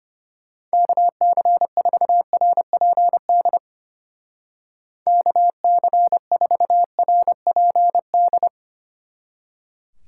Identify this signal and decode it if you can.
Morse code, heard as a keyed tone, about 32 wpm, 700 Hz. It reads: KC4RPD KC4RPD